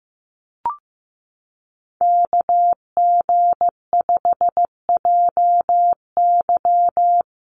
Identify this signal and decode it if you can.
Morse code, heard as a keyed tone, about 15 wpm, 700 Hz. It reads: KG5JY
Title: USA Call Signs Repeated 3x 15wpm